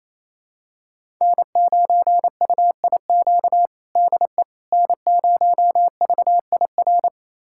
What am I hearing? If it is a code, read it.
Morse code, heard as a keyed tone, about 28 wpm, 700 Hz. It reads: N9UIQ DE N0VIR
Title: USA Callsign DE Callsign - 5 Characters - 28wpm